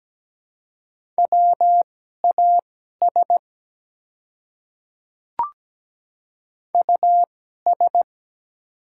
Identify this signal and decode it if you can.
Morse code, heard as a keyed tone, about 17 wpm, spaced as for 12 wpm, 700 Hz. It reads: WAS US
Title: 2x Character Spacing - Top 200 Words - 17wpm